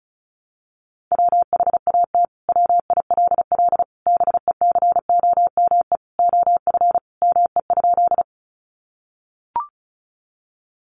Morse code, heard as a keyed tone, about 34 wpm, 700 Hz. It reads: WHAT WILL BECOME OF ME?
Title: Sentences from Top 600 Words 35wpm